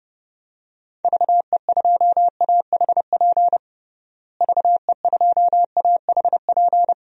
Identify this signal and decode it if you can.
Morse code, heard as a keyed tone, about 32 wpm, 700 Hz. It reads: VE2AHP VE2AHP
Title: DXCC-100 Call Signs - 6 Characters - Repeated 2x - 30wpm